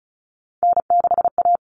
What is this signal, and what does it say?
Morse code, heard as a keyed tone, about 34 wpm, 700 Hz. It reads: N6A
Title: All Cut Numbers - 3-Digit 35wpm